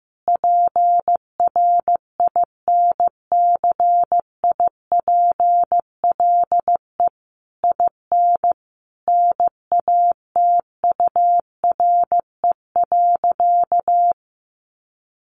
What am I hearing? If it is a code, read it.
Morse code, heard as a keyed tone, about 15 wpm, 700 Hz. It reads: PRINCIPLE IN NATURE.
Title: Sentences from Top 1400 Words 15wpm